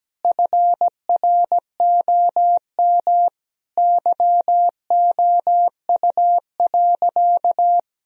Morse code, heard as a keyed tone, about 17 wpm, 700 Hz. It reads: FROM YOU.
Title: Sentences from Top 1000 Words - Review Part 1 - 17wpm